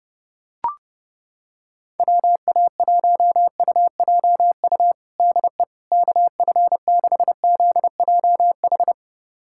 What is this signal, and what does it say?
Morse code, heard as a keyed tone, about 30 wpm, 700 Hz. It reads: WA1UJU DE KF6ZJH
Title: USA Callsign DE Callsign 30wpm